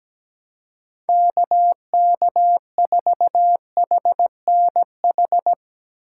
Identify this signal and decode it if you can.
Morse code, heard as a keyed tone, about 17 wpm, 700 Hz. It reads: KK4HNH
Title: Speed Racing - USA 2x3 Call Signs 17wpm